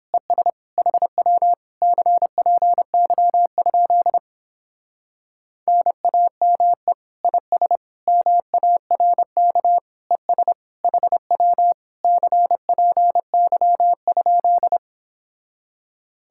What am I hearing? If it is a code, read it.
Morse code, heard as a keyed tone, about 27 wpm, 700 Hz. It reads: ES HW CPY? NAME IS MARK ES HW CPY?
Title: Speed-Racing Common QSO Phrases Encore - 22wpm